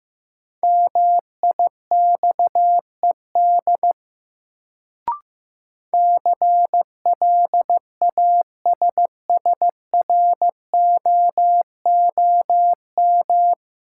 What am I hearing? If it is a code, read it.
Morse code, heard as a keyed tone, about 15 wpm, 700 Hz. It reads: MIXED CLASSROOM
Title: Sentences from Top 1500 Words - New Word Drill Alt - 15wpm